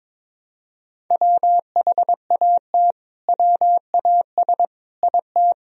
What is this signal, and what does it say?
Morse code, heard as a keyed tone, about 22 wpm, 700 Hz. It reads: WHAT WAS IT
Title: Sentences from Top 100 Words and Abbreviations 22wpm